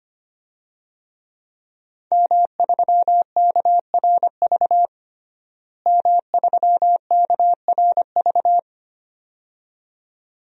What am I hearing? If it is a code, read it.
Morse code, heard as a keyed tone, about 25 wpm, 700 Hz. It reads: M3KRV M3KRV
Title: DXCC-100 Call Signs - 5 Characters - Repeated 2x - 25wpm